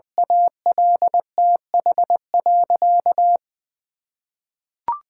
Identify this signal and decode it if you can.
Morse code, heard as a keyed tone, about 21 wpm, 700 Hz. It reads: ALTH.
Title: Sentences from Top 1000 Words - Review Part 6 - 20wpm